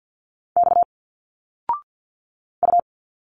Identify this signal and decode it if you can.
Morse code, heard as a keyed tone, about 50 wpm, 700 Hz. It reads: X U
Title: Single Letters 50wpm